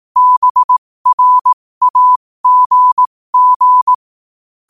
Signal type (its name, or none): Morse code